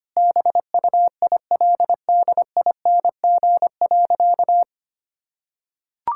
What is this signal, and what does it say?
Morse code, heard as a keyed tone, about 25 wpm, 700 Hz. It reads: BUILDING.
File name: practice / 25wpm / Sentences with 3-Word Phrases - 25wpm